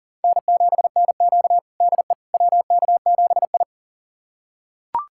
Morse code, heard as a keyed tone, about 40 wpm, 700 Hz. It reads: N7NQ DE WK7I
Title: USA Callsign DE Callsign - 4 Characters - 40wpm